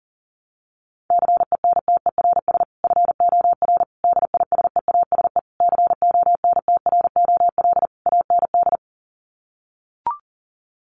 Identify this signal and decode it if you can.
Morse code, heard as a keyed tone, about 40 wpm, 700 Hz. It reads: CENTERS FOR DISEASE CONTROL AND
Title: Sets of 5 Words Encore2 40wpm